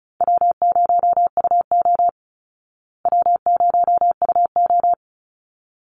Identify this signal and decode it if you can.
Morse code, heard as a keyed tone, about 34 wpm, 700 Hz. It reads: W0UO W0UO
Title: Easy Call Signs Repeated 2x - 35wpm